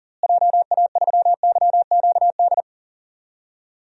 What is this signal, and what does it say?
Morse code, heard as a keyed tone, about 39 wpm, 700 Hz. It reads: JA3YQD
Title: DXCC-100 Call Signs - 6 Characters - Repeated 3x - 40wpm